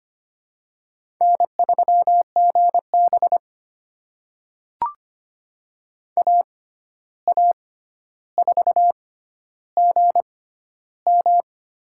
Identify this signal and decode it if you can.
Morse code, heard as a keyed tone, about 25 wpm, spaced as for 12 wpm, 700 Hz. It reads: N3GB AA4GM
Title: USA Call Signs - 6x Character Spacing - 25wpm